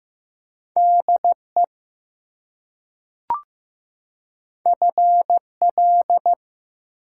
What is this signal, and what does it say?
Morse code, heard as a keyed tone, about 15 wpm, 700 Hz. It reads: DE FL
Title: US States and Canadian Provinces 15wpm